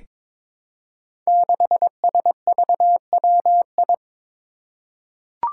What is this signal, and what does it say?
Morse code, heard as a keyed tone, about 21 wpm, 700 Hz. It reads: 6SVWI
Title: Five Challenging Letter-Number Combinations 22wpm